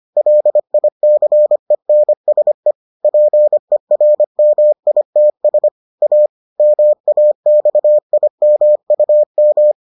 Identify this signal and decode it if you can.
Morse code, heard as a keyed tone, about 25 wpm, 600 Hz. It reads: LICENSE PERMITS A MAXIMUM